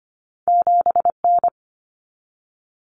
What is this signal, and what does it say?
Morse code, heard as a keyed tone, about 25 wpm, 700 Hz. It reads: 7N
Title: Common Cut Numbers - 2-Digit 25wpm